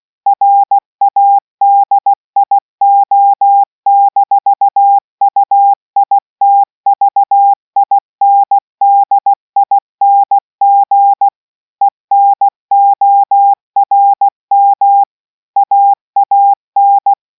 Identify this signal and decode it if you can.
Morse code, heard as a keyed tone, about 16 wpm, 800 Hz. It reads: RADIO-UITVINDING ENORM AAN